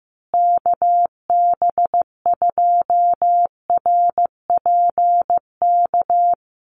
Morse code, heard as a keyed tone, about 15 wpm, 700 Hz. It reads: KB2RPK
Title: USA Call Signs Repeated 2x 15wpm